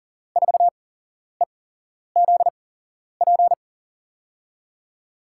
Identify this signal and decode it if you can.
Morse code, heard as a keyed tone, about 40 wpm, spaced as for 11 wpm, 700 Hz. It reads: 4EZP
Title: USA Call Signs - 6 Characters - 8x Character Spacing - 40wpm